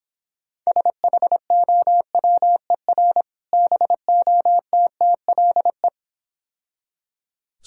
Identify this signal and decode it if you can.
Morse code, heard as a keyed tone, about 26 wpm, 700 Hz. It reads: SHOWER BOTTLE